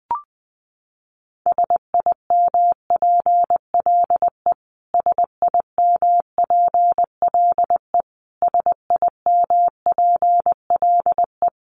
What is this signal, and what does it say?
Morse code, heard as a keyed tone, about 20 wpm, 700 Hz. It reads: SIMPLE SIMPLE SIMPLE
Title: Top 500 Words Repeated 20wpm